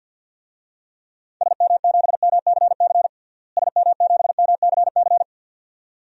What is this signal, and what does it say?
Morse code, heard as a keyed tone, about 47 wpm, 700 Hz. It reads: SM7MCC SM7MCC
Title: DXCC-100 Call Signs - 6 Characters - Repeated 2x - 50wpm